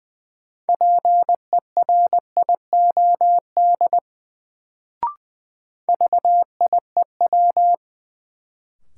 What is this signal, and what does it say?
Morse code, heard as a keyed tone, about 20 wpm, 700 Hz. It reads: PERIOD VIEW
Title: Sets of 5 Words - New Word Drill - 20wpm